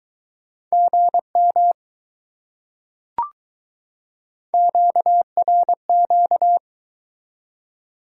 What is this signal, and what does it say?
Morse code, heard as a keyed tone, about 21 wpm, 700 Hz. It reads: GM QRQ